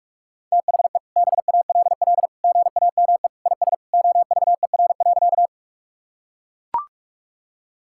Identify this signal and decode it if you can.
Morse code, heard as a keyed tone, about 41 wpm, 700 Hz. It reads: THE BALL GAME IS OVER.
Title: Sentences from Top 1000 Words - Review Part 6 - 45wpm